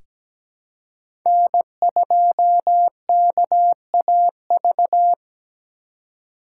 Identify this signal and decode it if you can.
Morse code, heard as a keyed tone, about 17 wpm, 700 Hz. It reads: N2KAV
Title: USA Call Signs - 5 Characters - 3x Character Spacing - 17wpm